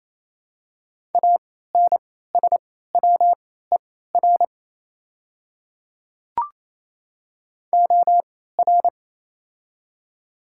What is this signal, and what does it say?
Morse code, heard as a keyed tone, about 28 wpm, spaced as for 16 wpm, 700 Hz. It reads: ANSWER OR
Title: 3x Character Spacing - Top 300 Words - 28wpm